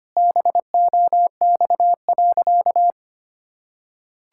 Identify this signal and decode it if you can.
Morse code, heard as a keyed tone, about 25 wpm, 700 Hz. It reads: BOX.
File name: practice / 25wpm / Sentences with 4-Word Phrases - 25wpm